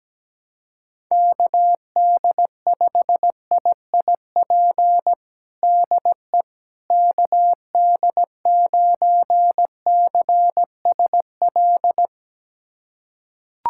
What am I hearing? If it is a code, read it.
Morse code, heard as a keyed tone, about 17 wpm, 700 Hz. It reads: KD5IIP DE KD9CSL
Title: USA Callsign DE Callsign - 6 Characters - 17wpm